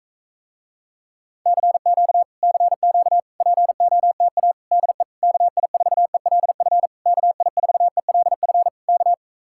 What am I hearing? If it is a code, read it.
Morse code, heard as a keyed tone, about 45 wpm, 700 Hz. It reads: CQ CQ POTA DE KI4ELF KI4ELF K